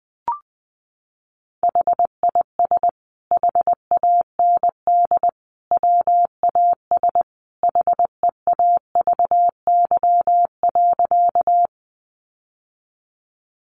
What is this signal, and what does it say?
Morse code, heard as a keyed tone, about 20 wpm, 700 Hz. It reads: HIS HAND WAS HEAVY.
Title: Sentences from Top 1100 Words 20wpm